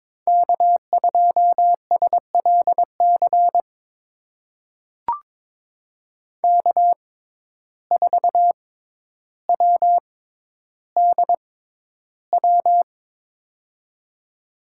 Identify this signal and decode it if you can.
Morse code, heard as a keyed tone, about 22 wpm, spaced as for 11 wpm, 700 Hz. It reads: K2SLC K4WDW
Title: USA Call Signs - 5 Characters - 6x Character Spacing - 22wpm